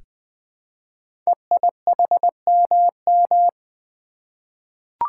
Morse code, heard as a keyed tone, about 20 wpm, 700 Hz. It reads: EIHMM
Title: Code Groups - 5 Letters - 20wpm